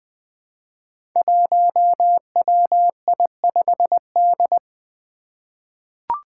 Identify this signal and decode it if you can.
Morse code, heard as a keyed tone, about 20 wpm, 700 Hz. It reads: 1WI5D
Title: Five Challenging Letter-Number Combinations 20wpm